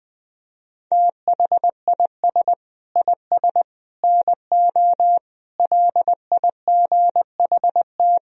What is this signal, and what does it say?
Morse code, heard as a keyed tone, about 20 wpm, 700 Hz. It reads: THIS IS NO LIGHT
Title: Sentences from Top 1000 Words - Review Part 1 - 20wpm